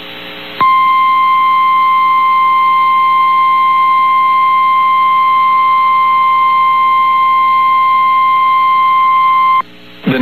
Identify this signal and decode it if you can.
Emergency Alert System (SAME) tones (the 1050 Hz attention tone). It is heard from 1.0 to 9.6 s.
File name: eas received alerts / L1 19520 audio